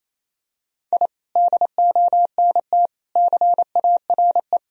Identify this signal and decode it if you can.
Morse code, heard as a keyed tone, about 28 wpm, 700 Hz. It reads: I DONT CARE
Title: Sentences from Top 1000 Words - Review Part 4 - 28wpm